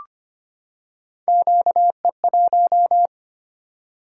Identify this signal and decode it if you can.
Morse code, heard as a keyed tone, about 25 wpm, 700 Hz. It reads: QE1
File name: Three Letter-Number Combinations - 25wpm